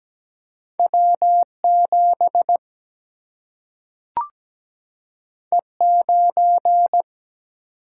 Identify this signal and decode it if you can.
Morse code, heard as a keyed tone, about 17 wpm, 700 Hz. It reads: W 7 E 9
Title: Single-Letter Single-Number 17wpm